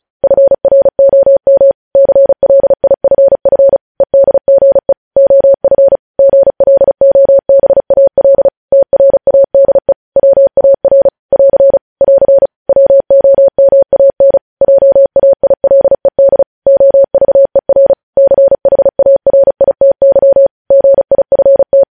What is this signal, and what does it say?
Morse code, heard as a keyed tone, about 36 wpm, 550 Hz. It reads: FROM CLIFF EDGE OF GLOBAL TRADE WAR <AR> <AR> WOMAN JAILED OVER CHARITY GIFT
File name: podcast / News headlines-1x-35x35